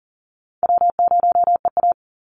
Morse code, heard as a keyed tone, about 40 wpm, 700 Hz. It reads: W0EA
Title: Call Signs Repeated 2x - 40wpm